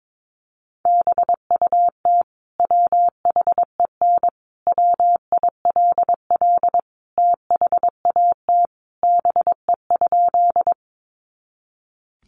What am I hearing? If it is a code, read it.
Morse code, heard as a keyed tone, about 22 wpm, 700 Hz. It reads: BUT WHEN WILL THAT BE?
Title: Sentences from Top 100 Words 22wpm